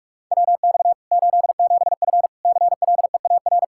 Morse code, heard as a keyed tone, about 46 wpm, 700 Hz. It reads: WX 87F CLEAR
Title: Practice QSO - Exchange 2 - Info - 45wpm